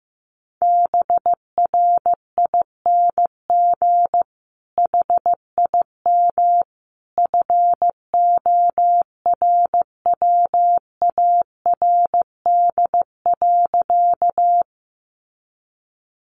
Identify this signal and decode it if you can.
Morse code, heard as a keyed tone, about 14 wpm, 700 Hz. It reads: BRING HIM FORWARD.
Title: Sentences from Top 1300 Words 15wpm